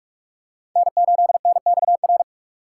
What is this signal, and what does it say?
Morse code, heard as a keyed tone, about 42 wpm, 700 Hz. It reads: N8NXR
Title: USA Call Signs - 6x Character Spacing - 45wpm